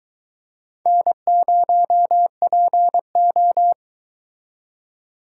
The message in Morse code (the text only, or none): N0PO